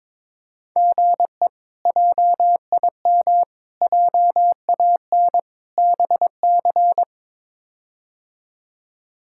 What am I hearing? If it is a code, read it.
Morse code, heard as a keyed tone, about 22 wpm, 700 Hz. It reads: GE JIM JAN BC
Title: K1USN SST QSO Practice - 22wpm